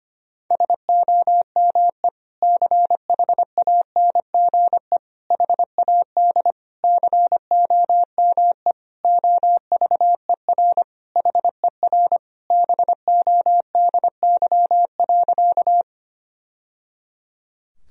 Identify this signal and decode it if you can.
Morse code, heard as a keyed tone, about 25 wpm, 700 Hz. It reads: SOME CHANGE HAD COME OVER HER BODY.
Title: Sentences from Top 1000 Words - Review Part 2 - 25wpm